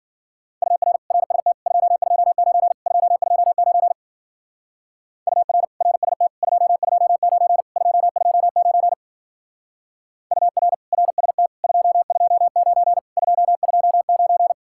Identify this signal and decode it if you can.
Morse code, heard as a keyed tone, about 48 wpm, 700 Hz. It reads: UR RST 229 229 UR RST 229 229 UR RST 229 229
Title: Speed-Racing Common QSO Phrases - 40wpm